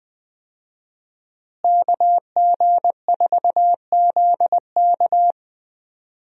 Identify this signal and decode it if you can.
Morse code, heard as a keyed tone, about 20 wpm, 700 Hz. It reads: KG4ZK
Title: USA Call Signs - 5 Characters - Repeated 3x - 20wpm